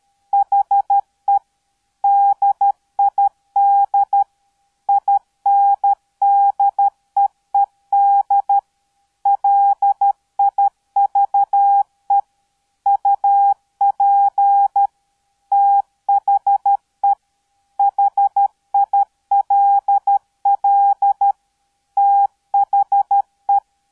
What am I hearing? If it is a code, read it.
Morse code, heard as a keyed tone, about 13 wpm, 800 Hz. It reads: HE DID INDEED LIVE UP THE HILL THE